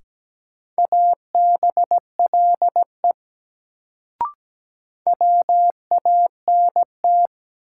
Morse code, heard as a keyed tone, about 17 wpm, 700 Hz. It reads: ABLE WANT
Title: Sets of 2 Words - New Word Drill - 17wpm